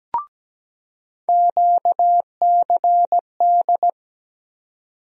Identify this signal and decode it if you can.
Morse code, heard as a keyed tone, about 17 wpm, 700 Hz. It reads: QCD
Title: USA Call Sign Postfixes - 3 Letters 17wpm